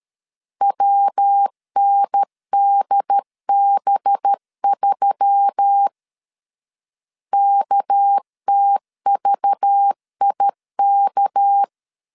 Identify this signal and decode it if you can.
Morse code, heard as a keyed tone, about 13 wpm, 800 Hz. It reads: WNDB3 KTVIK
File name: LEZIONI / 6 PKX0 / 6 PK 12wpm 2006